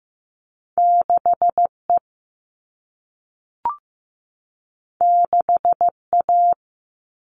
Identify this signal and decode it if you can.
Morse code, heard as a keyed tone, about 15 wpm, 700 Hz. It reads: 6E 6A
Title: All Cut Numbers - 2-Digit 15wpm